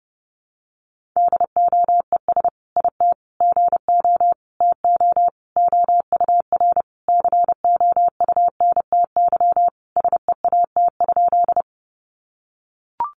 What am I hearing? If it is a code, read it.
Morse code, heard as a keyed tone, about 29 wpm, 700 Hz. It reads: DOES IT GO TO OUR COUNTY SEAT?
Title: Sentences from Top 1500 Words 30wpm